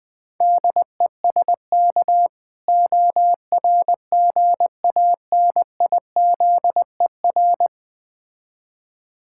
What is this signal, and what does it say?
Morse code, heard as a keyed tone, about 20 wpm, 700 Hz. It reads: DESK ORGANIZER
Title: Difficult Words with Hints - 20wpm